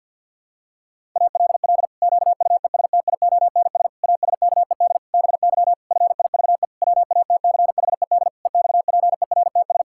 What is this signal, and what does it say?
Morse code, heard as a keyed tone, about 51 wpm, 700 Hz. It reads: ALL QUESTIONS ASKED BY FIVE WATCHED EXPERTS
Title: Pangram Challenge v2 - 50wpm